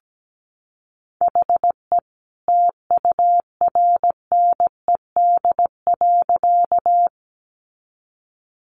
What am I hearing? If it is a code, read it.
Morse code, heard as a keyed tone, about 17 wpm, 700 Hz. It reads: HE TURNED.
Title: A Sound Of Thunder 17wpm